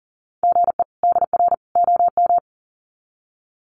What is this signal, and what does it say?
Morse code, heard as a keyed tone, about 40 wpm, 700 Hz. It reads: GE DR OM
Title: Common QSO Phrases 40wpm